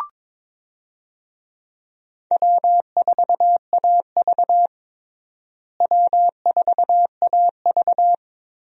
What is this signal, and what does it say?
Morse code, heard as a keyed tone, about 22 wpm, 700 Hz. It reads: W4AV W4AV
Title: USA Call Signs - 4 Characters - Repeated 3x - 22wpm